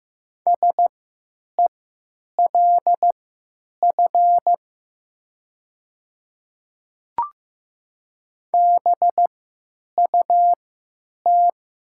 Morse code, heard as a keyed tone, about 15 wpm, spaced as for 9 wpm, 700 Hz. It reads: SELF BUT
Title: 3x Character Spacing - Top 300 Words - 15wpm